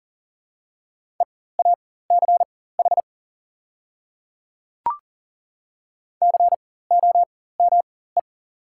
Morse code, heard as a keyed tone, about 37 wpm, spaced as for 18 wpm, 700 Hz. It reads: EACH COME
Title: 4x Character Spacing - Top 100 Words - 40wpm